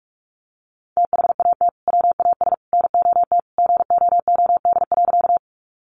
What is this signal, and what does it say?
Morse code, heard as a keyed tone, about 47 wpm, 700 Hz. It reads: THAT WAS NOT GOOD.
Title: Sentences from Top 600 Words 45wpm